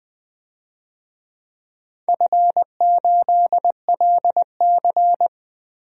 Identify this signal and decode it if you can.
Morse code, heard as a keyed tone, about 20 wpm, 700 Hz. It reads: F8LC
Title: DXCC-100 Call Signs - 4 Characters - 20wpm